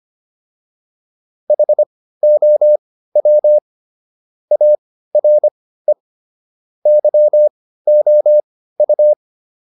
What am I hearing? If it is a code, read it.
Morse code, heard as a keyed tone, about 25 wpm, spaced as for 15 wpm, 600 Hz. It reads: HOW ARE YOU